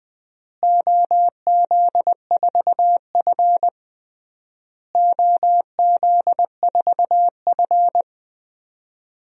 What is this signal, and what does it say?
Morse code, heard as a keyed tone, about 20 wpm, 700 Hz. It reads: OZ4F OZ4F
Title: DXCC-100 Call Signs - 4 Characters - Repeated 3x - 20wpm